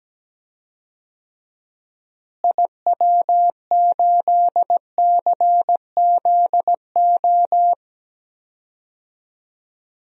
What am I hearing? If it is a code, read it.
Morse code, heard as a keyed tone, about 17 wpm, 700 Hz. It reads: IW8CZO
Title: DXCC-100 Call Signs - 6 Characters - Repeated 2x - 17wpm